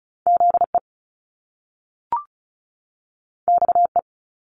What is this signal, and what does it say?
Morse code, heard as a keyed tone, about 36 wpm, 700 Hz. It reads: ZE XE
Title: Top Two Letter Combinations in Words 35wpm